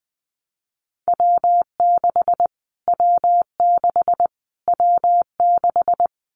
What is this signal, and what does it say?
Morse code, heard as a keyed tone, about 20 wpm, 700 Hz. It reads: W6 W6 W6
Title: Callsign Prefixes for Top 25 DXCC - Repeated 3x - 20wpm